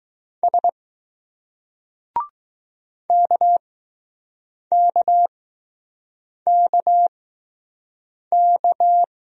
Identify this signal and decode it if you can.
Morse code, heard as a keyed tone, about 20 wpm, 700 Hz. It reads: S K K K K